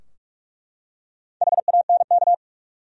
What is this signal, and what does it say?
Morse code, heard as a keyed tone, about 47 wpm, 700 Hz. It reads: HANK
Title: USA Common Names - Max 4 Letters - 45wpm